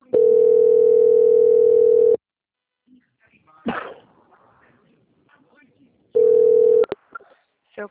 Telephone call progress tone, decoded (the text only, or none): ringback tone